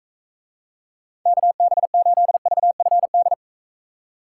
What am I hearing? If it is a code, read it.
Morse code, heard as a keyed tone, about 42 wpm, 700 Hz. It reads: KB8VFD